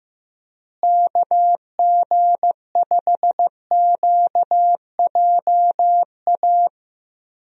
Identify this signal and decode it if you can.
Morse code, heard as a keyed tone, about 15 wpm, 700 Hz. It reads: KG5QJA
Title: USA Call Signs - 6 Characters - Repeated 3x - 15wpm